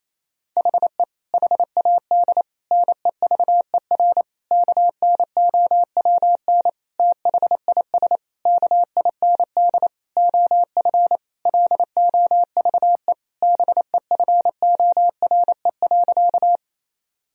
Morse code, heard as a keyed tone, about 28 wpm, 700 Hz. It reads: HE HAD NEVER KNOWN THIS KIND OF LOVE BEFORE.
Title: Sentences from Top 1000 Words - Review Part 3 - 28wpm